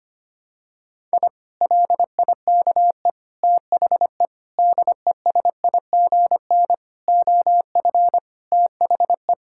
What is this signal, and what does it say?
Morse code, heard as a keyed tone, about 25 wpm, 700 Hz. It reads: I LIKE THE DESIGN OF THE
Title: Sentences from Top 1000 Words - Review Part 5 - 25wpm